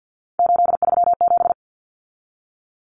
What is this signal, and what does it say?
Morse code, heard as a keyed tone, about 50 wpm, 700 Hz. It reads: 837
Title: Numbers - 3-Digit 50wpm